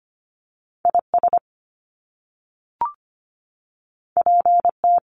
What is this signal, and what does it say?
Morse code, heard as a keyed tone, about 25 wpm, 700 Hz. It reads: IS PT
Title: Top Two Letter Combinations in Words 25wpm